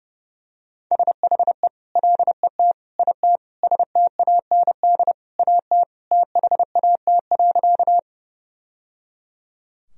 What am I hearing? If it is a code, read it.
Morse code, heard as a keyed tone, about 32 wpm, 700 Hz. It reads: SHE LET IT STAND AT THAT.
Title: Sentences from Top 1000 Words - Review Part 1 - 30wpm